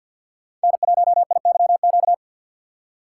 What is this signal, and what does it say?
Morse code, heard as a keyed tone, about 50 wpm, 700 Hz. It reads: N1IYQ
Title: USA Call Signs - 5 Characters - 2x Character Spacing - 50wpm